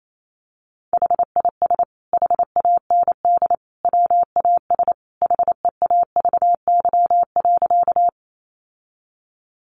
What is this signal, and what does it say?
Morse code, heard as a keyed tone, about 28 wpm, 700 Hz. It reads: HIS HAND WAS HEAVY.
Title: Sentences from Top 1100 Words 28wpm